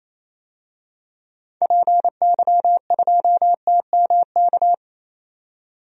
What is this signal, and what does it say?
Morse code, heard as a keyed tone, about 28 wpm, 700 Hz. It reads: PY2TMK
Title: DXCC-100 Call Signs - 6 Characters - 28wpm